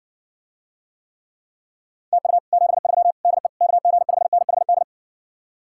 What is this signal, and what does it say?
Morse code, heard as a keyed tone, about 60 wpm, 700 Hz. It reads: TU 73 DE KG5NHD